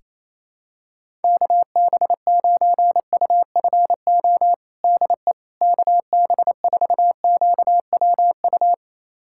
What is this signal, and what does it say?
Morse code, heard as a keyed tone, about 28 wpm, 700 Hz. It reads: KB9UFO DE KB4QWU
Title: USA Callsign DE Callsign - 6 Characters - 28wpm